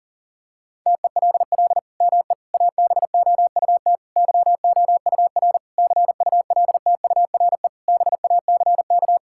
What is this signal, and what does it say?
Morse code, heard as a keyed tone, about 40 wpm, 700 Hz. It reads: TELL ME ABOUT YOUR CULTURE BACK
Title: Sentences from Top 1000 Words - Review Part 6 - 40wpm